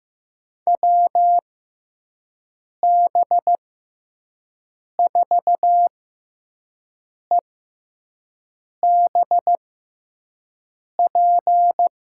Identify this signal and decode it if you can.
Morse code, heard as a keyed tone, about 15 wpm, spaced as for 5 wpm, 700 Hz. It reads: WB4EBP
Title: USA Call Signs - 6x Character Spacing - 15wpm